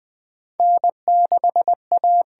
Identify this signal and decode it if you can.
Morse code, heard as a keyed tone, about 20 wpm, 700 Hz. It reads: N6A
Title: USA Call Signs - 4 Characters - 8x Character Spacing - 20wpm